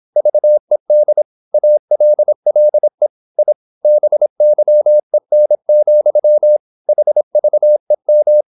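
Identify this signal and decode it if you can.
Morse code, heard as a keyed tone, about 26 wpm, 600 Hz. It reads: VED ALLE I BYEN, HVEM
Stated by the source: High-speed (QRQ) Morse at 26 wpm.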